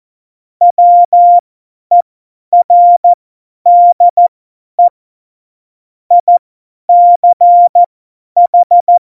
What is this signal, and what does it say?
Morse code, heard as a keyed tone, about 14 wpm, spaced as for 10 wpm, 700 Hz. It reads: WERDE ICH
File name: Morsekurs / Witze / Witz 23. April 2019, 50 F70